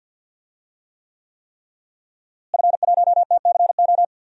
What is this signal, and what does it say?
Morse code, heard as a keyed tone, about 50 wpm, 700 Hz. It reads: F1TCO
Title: DXCC-100 Call Signs - 5 Characters - Repeated 3x - 50wpm